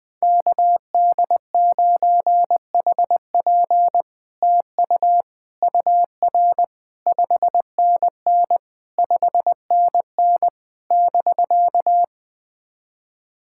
Morse code, heard as a keyed tone, about 20 wpm, 700 Hz. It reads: KD9HP TU UR 5NN 5NN <BK>
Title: SOTA QSO Practice 20wpm